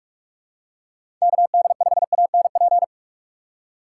Morse code, heard as a keyed tone, about 46 wpm, 700 Hz. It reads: KD5ANP
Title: USA Call Signs - 6 Characters - 4x Character Spacing - 45wpm